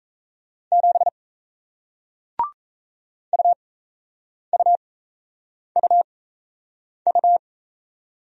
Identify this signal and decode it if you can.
Morse code, heard as a keyed tone, about 38 wpm, spaced as for 8 wpm, 700 Hz. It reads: 7 UUUU